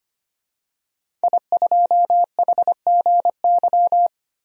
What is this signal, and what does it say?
Morse code, heard as a keyed tone, about 25 wpm, 700 Hz. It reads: I2HGY
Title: DXCC-100 Call Signs - 5 Characters - 25wpm